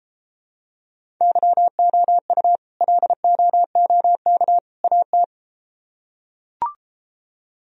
Morse code, heard as a keyed tone, about 33 wpm, 700 Hz. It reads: YOU LOOK AT